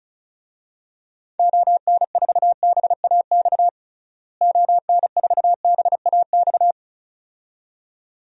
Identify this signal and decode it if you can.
Morse code, heard as a keyed tone, about 36 wpm, 700 Hz. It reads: ON4BAX ON4BAX
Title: DXCC-100 Call Signs - 6 Characters - Repeated 2x - 35wpm